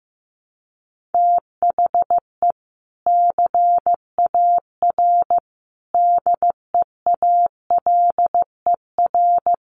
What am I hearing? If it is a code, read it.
Morse code, heard as a keyed tone, about 15 wpm, 700 Hz. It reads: THE CAR DEALER
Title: Sentences from Top 1100 Words 15wpm